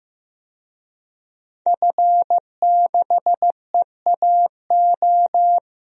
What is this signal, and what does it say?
Morse code, heard as a keyed tone, about 15 wpm, 700 Hz. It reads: F6EAO
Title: DXCC-100 Call Signs - 5 Characters - Repeated 2x - 15wpm